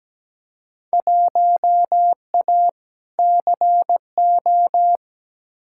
Morse code, heard as a keyed tone, about 17 wpm, 700 Hz. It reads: 1A CO
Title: Field Day Exchanges - 17wpm